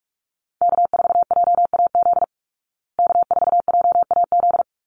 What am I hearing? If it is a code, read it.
Morse code, heard as a keyed tone, about 44 wpm, 700 Hz. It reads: K4JAZ K4JAZ
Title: Call Signs Repeated 3x - 45wpm